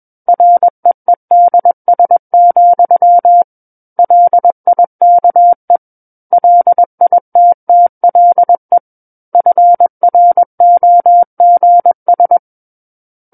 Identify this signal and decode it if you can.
Morse code, heard as a keyed tone, about 21 wpm, 700 Hz. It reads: REEDS, LIKE LITTLE FROGS
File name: Book One / 12-Chapter 12 - 21 WPM